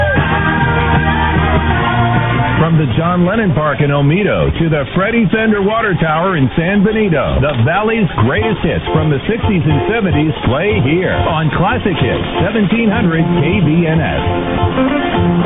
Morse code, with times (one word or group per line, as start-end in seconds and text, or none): none